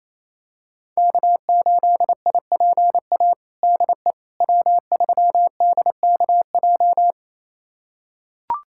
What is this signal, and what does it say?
Morse code, heard as a keyed tone, about 27 wpm, 700 Hz. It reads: K8IPA DE W3DKJ
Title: USA Callsign DE Callsign - 5 Characters - 28wpm